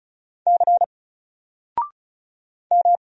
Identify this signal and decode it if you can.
Morse code, heard as a keyed tone, about 34 wpm, 700 Hz. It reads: C M
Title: Single Letter-Number 35wpm